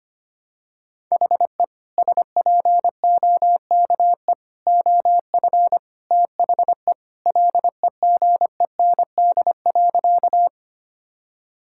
Morse code, heard as a keyed tone, about 25 wpm, 700 Hz. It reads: HE SPOKE OF THE LEGEND.
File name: practice / 25wpm / Sentences with 2-Word Phrases - 25wpm